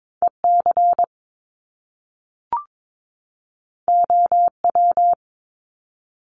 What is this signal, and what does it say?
Morse code, heard as a keyed tone, about 22 wpm, 700 Hz. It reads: EC OW
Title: Top Two Letter Combinations in Words 22wpm